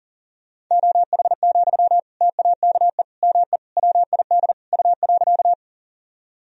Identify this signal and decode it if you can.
Morse code, heard as a keyed tone, about 37 wpm, 700 Hz. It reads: OH, TAKE ME WID U.
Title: Sentences from Top 500 Words and Abbreviations 40wpm